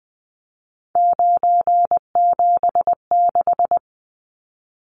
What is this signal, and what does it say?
Morse code, heard as a keyed tone, about 19 wpm, 700 Hz. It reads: 976
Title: Numbers - 3-Digit 20wpm